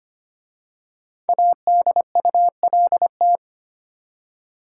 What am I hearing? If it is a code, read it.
Morse code, heard as a keyed tone, about 25 wpm, 700 Hz. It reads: ADULT
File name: 5 Letter Words - First 3 Letters Given - 25wpm